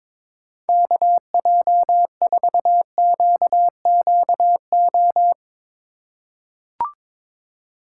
Morse code, heard as a keyed tone, about 21 wpm, 700 Hz. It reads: KJ4QQO
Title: USA Call Signs - 6 Characters - 4x Character Spacing - 22wpm